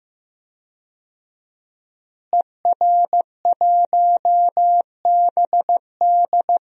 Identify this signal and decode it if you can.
Morse code, heard as a keyed tone, about 15 wpm, 700 Hz. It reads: ER1BD
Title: DXCC-100 Call Signs - 5 Characters - Repeated 3x - 15wpm